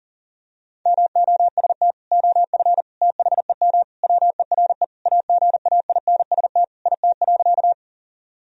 Morse code, heard as a keyed tone, about 37 wpm, 700 Hz. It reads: MOST OF THEM WERE AGAINST IT.
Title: Sentences from Top 1000 Words - Review Part 1 - 40wpm